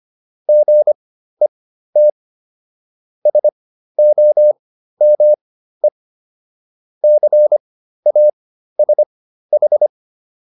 Morse code, heard as a keyed tone, about 25 wpm, spaced as for 13 wpm, 600 Hz. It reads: GET SOME CASH